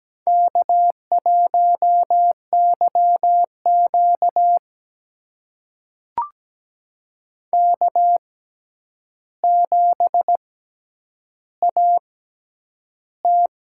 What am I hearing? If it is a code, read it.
Morse code, heard as a keyed tone, about 17 wpm, spaced as for 9 wpm, 700 Hz. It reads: K1YQ K7AT
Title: USA Call Signs - 4 Characters - 6x Character Spacing - 17wpm